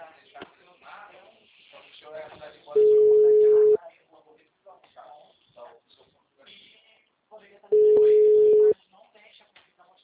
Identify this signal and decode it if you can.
Ringback tone (Europe, 425 Hz), with speech.
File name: out- -2047- - - .34169